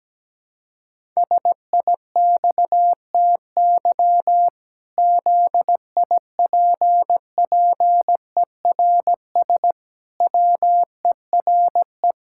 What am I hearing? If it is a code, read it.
Morse code, heard as a keyed tone, about 17 wpm, 700 Hz. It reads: SIXTY ZIPPERS WERE
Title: Pangram Challenge v2 - 17wpm